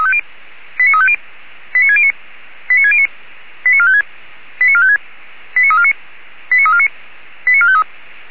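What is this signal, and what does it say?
Five-tone selective call (ZVEI), heard as quick runs of five tones, tones of about 70 ms. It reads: 89350 89609 89609 89456 89456 89349 89349 89453
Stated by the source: ZVEI five-tone selective call 89293 received on an SDR by Feuerwehr Orlamünde.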